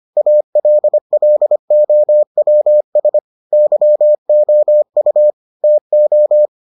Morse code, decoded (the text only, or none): ALLOWS YOU TO